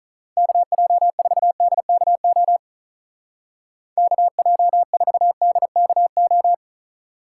Morse code, heard as a keyed tone, about 44 wpm, 700 Hz. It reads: KJ4DKO KJ4DKO